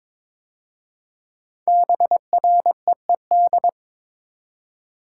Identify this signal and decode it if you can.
Morse code, heard as a keyed tone, about 22 wpm, 700 Hz. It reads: BREED
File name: 5 Letter Words - First 3 Letters Given - 22wpm